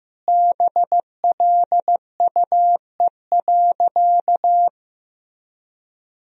Morse code, heard as a keyed tone, about 15 wpm, 700 Hz. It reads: BLUE.
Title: Sentences from Top 1000 Words - Review Part 3 - 15wpm